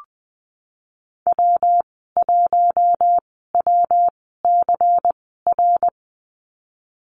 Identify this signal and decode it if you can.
Morse code, heard as a keyed tone, about 20 wpm, spaced as for 14 wpm, 700 Hz. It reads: W1WCR
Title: 147 Lesson 24g - Letter y - Call Signs 20wpm